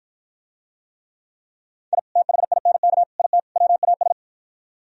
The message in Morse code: I THINK IT WAS

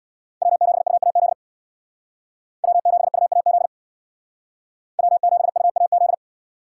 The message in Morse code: W7UAZ W7UAZ W7UAZ

